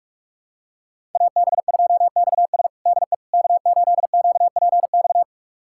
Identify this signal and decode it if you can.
Morse code, heard as a keyed tone, about 46 wpm, 700 Hz. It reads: AB2XS DE K8QPX
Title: USA Callsign DE Callsign - 5 Characters - 45wpm